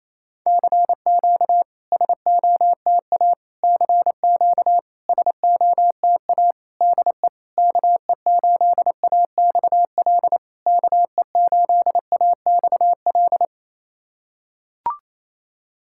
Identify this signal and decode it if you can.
Morse code, heard as a keyed tone, about 27 wpm, 690 Hz. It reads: CQ SOTA CQ SOTA DE KE8AXL KE8AXL
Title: SOTA QSO Practice 28wpm